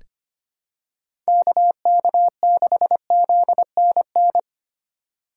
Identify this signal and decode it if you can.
Morse code, heard as a keyed tone, about 25 wpm, 700 Hz. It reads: KK6ZNN
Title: USA Call Signs - 6 Characters - 3x Character Spacing - 25wpm